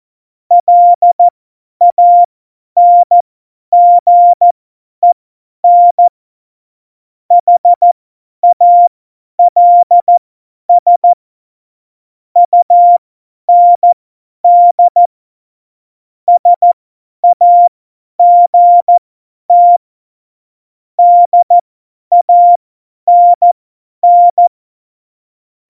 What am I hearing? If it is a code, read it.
Morse code, heard as a keyed tone, about 14 wpm, spaced as for 10 wpm, 700 Hz. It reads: LANGEN HALS UND SAGT DANN
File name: Morsekurs / Witze / Witz 15. Mai 2018, 50 F70